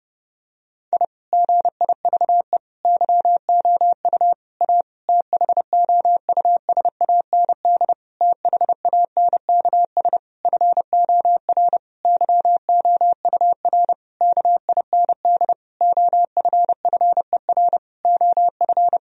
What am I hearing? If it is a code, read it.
Morse code, heard as a keyed tone, about 30 wpm, 700 Hz. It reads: I GIVE YOU A THOUSAND THANKS FOR YOUR KIND OFFER OF
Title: Sentences from Top 1000 Words - Review Part 4 - 30wpm